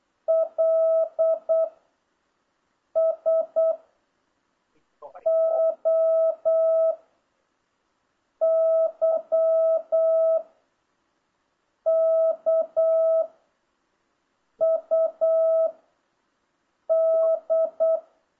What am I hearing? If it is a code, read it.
Morse code, heard as a keyed tone, about 8 wpm, 660 Hz. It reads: L S O Y K U D